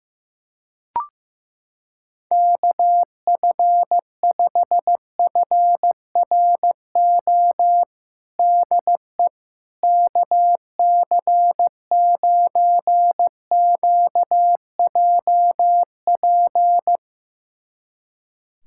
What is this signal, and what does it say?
Morse code, heard as a keyed tone, about 15 wpm, 700 Hz. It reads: KF5FRO DE KC9QJP
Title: USA Callsign DE Callsign - 6 Characters - 15wpm